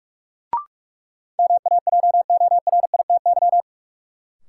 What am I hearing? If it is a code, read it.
Morse code, heard as a keyed tone, about 42 wpm, 700 Hz. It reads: MAJORITY
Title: Sets of 3 Words - New Word Drill - 45wpm